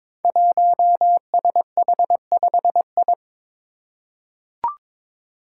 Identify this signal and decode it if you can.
Morse code, heard as a keyed tone, about 22 wpm, 700 Hz. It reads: 1SH5I
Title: Five Challenging Letter-Number Combinations 22wpm